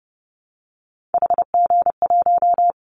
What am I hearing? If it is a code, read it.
Morse code, heard as a keyed tone, about 30 wpm, 700 Hz. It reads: HG1
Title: Callsign Prefixes for Top 25 DXCC - Repeated 3x - 30wpm